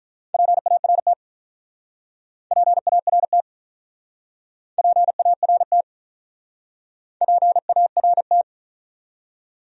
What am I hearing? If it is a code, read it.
Morse code, heard as a keyed tone, about 48 wpm, 700 Hz. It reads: PART PART PART PART